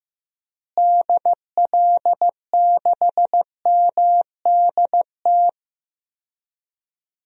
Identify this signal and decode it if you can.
Morse code, heard as a keyed tone, about 15 wpm, 700 Hz. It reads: DL6MDT
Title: DXCC-100 Call Signs - 6 Characters - Repeated 2x - 15wpm